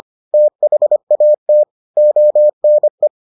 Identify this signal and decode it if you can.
Morse code, heard as a keyed tone, about 25 wpm, 600 Hz. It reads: THAT ONE